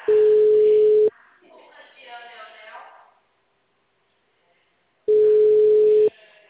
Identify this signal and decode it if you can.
Ringback tone (Europe, 425 Hz), with speech.